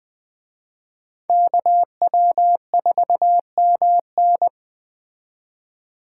Morse code, heard as a keyed tone, about 20 wpm, 700 Hz. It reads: KW4MN
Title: Speed Racing - USA 2x2 Call Signs 17wpm